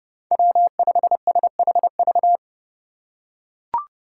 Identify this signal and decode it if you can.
Morse code, heard as a keyed tone, about 30 wpm, 700 Hz. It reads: W5SHV
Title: USA Call Signs - 2x Character Spacing - 30wpm